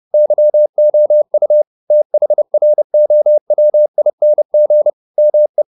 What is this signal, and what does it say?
Morse code, heard as a keyed tone, about 30 wpm, 600 Hz. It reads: YOU THROWING ME
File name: the old man and the sea / the old man and the sea 1 30wpm